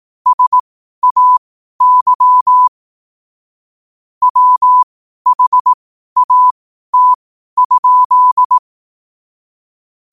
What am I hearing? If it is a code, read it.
Morse code, heard as a keyed tone, about 18 wpm, spaced as for 12 wpm, 1000 Hz. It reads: SAY WHAT?
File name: qotdfiles / 12wpm